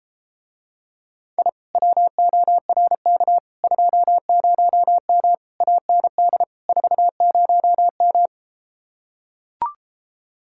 Morse code, heard as a keyed tone, about 33 wpm, 700 Hz. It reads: I WORK 20M AND 40M